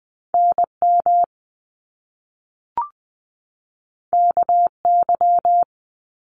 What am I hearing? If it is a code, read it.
Morse code, heard as a keyed tone, about 20 wpm, 700 Hz. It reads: NM KY
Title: US States 20wpm